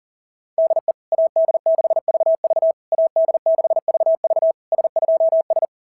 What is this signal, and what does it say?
Morse code, heard as a keyed tone, about 37 wpm, 690 Hz. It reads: DE AD6VV AD6VV S2S
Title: SOTA QSO Practice 40wpm